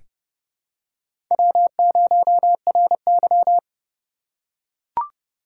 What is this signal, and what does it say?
Morse code, heard as a keyed tone, about 30 wpm, 700 Hz. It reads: W0RY
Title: USA Call Signs - 4 Characters - 6x Character Spacing - 30wpm